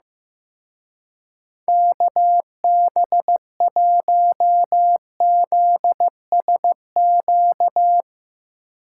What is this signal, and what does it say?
Morse code, heard as a keyed tone, about 16 wpm, 700 Hz. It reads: KB1ZSQ
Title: Speed Racing - USA 2x3 Call Signs 15wpm